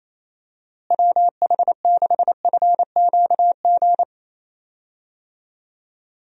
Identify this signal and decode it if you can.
Morse code, heard as a keyed tone, about 28 wpm, 700 Hz. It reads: WH6FQG
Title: USA Call Signs - 6 Characters - Repeated 3x - 28wpm